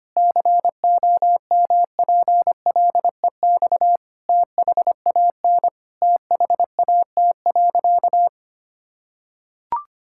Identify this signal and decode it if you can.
Morse code, heard as a keyed tone, about 25 wpm, 700 Hz. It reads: COMPLEX THAN THAT.
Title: Sentences from Top 1000 Words - Review Part 6 - 25wpm